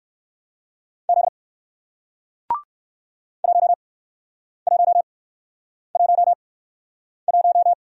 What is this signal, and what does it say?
Morse code, heard as a keyed tone, about 57 wpm, spaced as for 9 wpm, 700 Hz. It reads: Z 1111